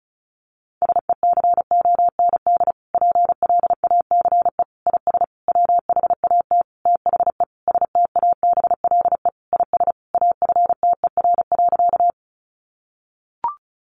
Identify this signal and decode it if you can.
Morse code, heard as a keyed tone, about 34 wpm, 700 Hz. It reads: SECOND PLACE IS WHAT THE STABLE IS AFTER.
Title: Sentences from Top 1500 Words 35wpm